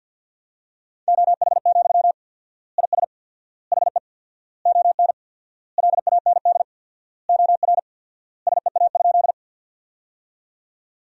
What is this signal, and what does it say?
Morse code, heard as a keyed tone, about 53 wpm, 700 Hz. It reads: OH, IS HE ON LAND OR SEA?
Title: 4x Word-Spacing - Sentences from Top 500 Words - 50wpm